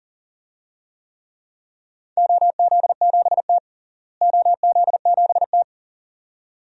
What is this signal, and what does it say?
Morse code, heard as a keyed tone, about 40 wpm, 700 Hz. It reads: OZ7T OZ7T
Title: DXCC-100 Call Signs - 4 Characters - Repeated 2x - 40wpm